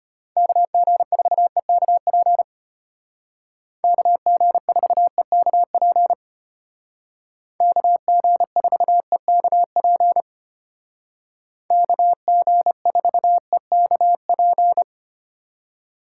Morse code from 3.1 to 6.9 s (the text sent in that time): KG4EKP